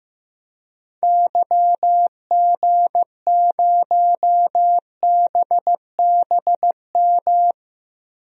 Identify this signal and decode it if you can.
Morse code, heard as a keyed tone, about 15 wpm, 700 Hz. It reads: YG0BBM
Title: DXCC-100 Call Signs - 6 Characters - Repeated 3x - 15wpm